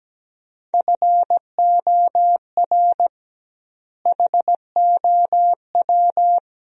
Morse code, heard as a keyed tone, about 17 wpm, 700 Hz. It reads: FOR HOW
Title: 2x Word-Spacing - Sentences from Top 200 Words - 17wpm